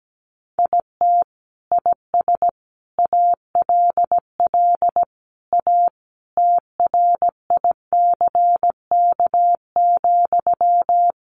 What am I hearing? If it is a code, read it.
Morse code, heard as a keyed tone, about 17 wpm, 700 Hz. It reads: IT IS ALL A TRICK,
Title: Sentences from Top 1500 Words 17wpm